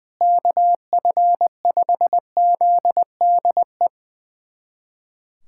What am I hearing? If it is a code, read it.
Morse code, heard as a keyed tone, about 20 wpm, 700 Hz. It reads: KF5ZDE
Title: USA Callsign DE Callsign 20wpm